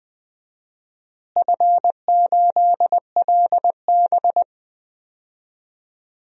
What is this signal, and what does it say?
Morse code, heard as a keyed tone, about 21 wpm, 700 Hz. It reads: F8LB
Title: DXCC-100 Call Signs - 4 Characters - Repeated 3x - 20wpm